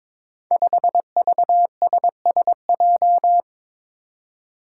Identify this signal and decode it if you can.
Morse code, heard as a keyed tone, about 22 wpm, 700 Hz. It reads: 5VSSJ
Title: Five Challenging Letter-Number Combinations 22wpm